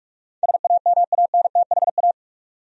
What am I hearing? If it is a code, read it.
Morse code, heard as a keyed tone, about 45 wpm, 700 Hz. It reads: SAMANTHA
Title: USA Common Names - 45wpm